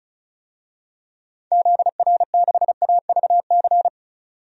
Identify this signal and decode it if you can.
Morse code, heard as a keyed tone, about 36 wpm, 690 Hz. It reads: ZR6AVC
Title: DXCC-100 Call Signs - 6 Characters - 35wpm